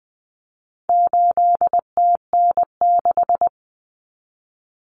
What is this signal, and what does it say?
Morse code, heard as a keyed tone, about 20 wpm, 700 Hz. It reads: 8TN6
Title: Common Cut Numbers - 4-Digit 20wpm